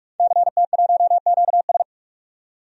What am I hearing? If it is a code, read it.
Morse code, heard as a keyed tone, about 46 wpm, 700 Hz. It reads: CT1QS
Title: DXCC-100 Call Signs - 5 Characters - 45wpm